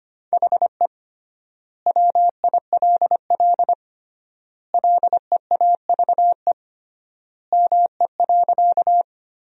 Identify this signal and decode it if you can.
Morse code, heard as a keyed tone, about 25 wpm, 700 Hz. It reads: HE WILL LEAVE ME.
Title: 3x Word-Spacing - Sentences from Top 400 Words - 25wpm